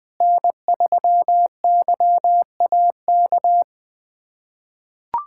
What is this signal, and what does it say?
Morse code, heard as a keyed tone, about 20 wpm, 700 Hz. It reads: N3YAK
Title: USA Callsign DE Callsign - 5 Characters - 20wpm